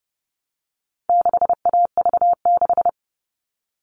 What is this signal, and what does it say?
Morse code, heard as a keyed tone, about 30 wpm, 700 Hz. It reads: 6AV6
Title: All Cut Numbers - 4-Digit 30wpm